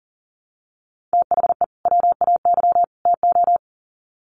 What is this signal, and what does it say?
Morse code, heard as a keyed tone, about 39 wpm, 700 Hz. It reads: THE WAY TO
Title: Sets of 3 Words 40wpm